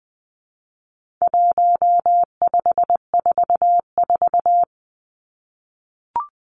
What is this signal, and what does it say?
Morse code, heard as a keyed tone, about 20 wpm, 700 Hz. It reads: 1544
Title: Numbers - 4-Digit 20wpm